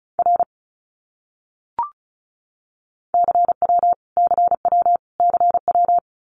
Morse code, heard as a keyed tone, about 34 wpm, 700 Hz. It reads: R CW CW CW
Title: Abbreviations Repeated 3x - 35wpm